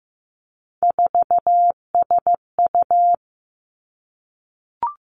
Morse code, heard as a keyed tone, about 15 wpm, 700 Hz. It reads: 4SU
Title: Three Challenging Letter-Number Combinations 15wpm